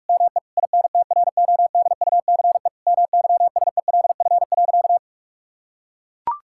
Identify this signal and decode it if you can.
Morse code, heard as a keyed tone, about 42 wpm, 700 Hz. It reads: ME INTRODUCE MYSELF.
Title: Sentences from Top 1600 Words 45wpm